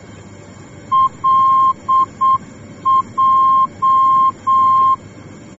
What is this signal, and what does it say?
Morse code, heard as a keyed tone, about 7 wpm, 1060 Hz. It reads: LJ
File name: germany / mx / LJ-D